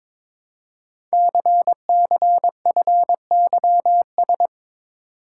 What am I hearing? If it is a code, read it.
Morse code, heard as a keyed tone, about 22 wpm, 700 Hz. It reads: CCFYS
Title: Code Groups - 5 Letters - 22wpm